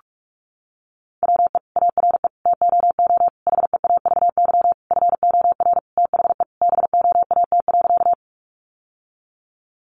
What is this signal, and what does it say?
Morse code, heard as a keyed tone, about 44 wpm, 700 Hz. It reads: WE ARE TOO HEAVY FOR THE BOAT.
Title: Sentences from Top 1100 Words 45wpm